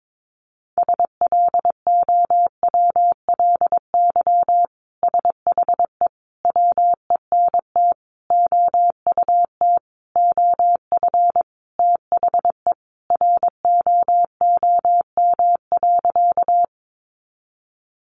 Morse code, heard as a keyed tone, about 22 wpm, 700 Hz. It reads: SLOWLY SHE WENT OUT OF THE ROOM.
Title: Sentences from Top 1100 Words 22wpm